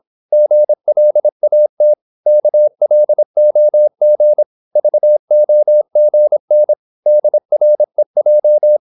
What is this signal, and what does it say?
Morse code, heard as a keyed tone, about 26 wpm, 600 Hz. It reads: GLAT KLOG VOGN DREJ
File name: Korte ord DK 26wpm